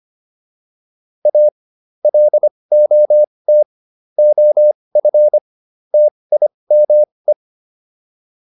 Morse code, heard as a keyed tone, about 25 wpm, spaced as for 20 wpm, 600 Hz. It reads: A LOT OF TIME